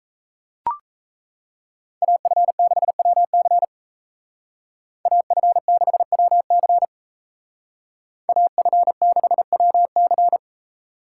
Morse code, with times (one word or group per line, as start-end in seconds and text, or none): 2.0-3.6 AF6WC
5.1-6.8 AF6WC
8.3-10.4 AF6WC